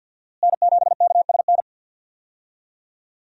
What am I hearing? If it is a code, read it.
Morse code, heard as a keyed tone, about 46 wpm, 700 Hz. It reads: N7KSN
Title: USA Call Signs - 5 Characters - Repeated 2x - 50wpm